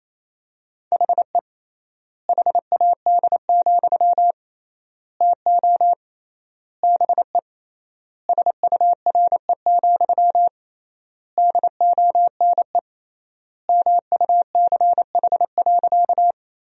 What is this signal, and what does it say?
Morse code, heard as a keyed tone, about 27 wpm, 700 Hz. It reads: HE HAD, TO BE SURE, DONE MUCH.
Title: 3x Word-Spacing - Sentences from Top 500 Words - 28wpm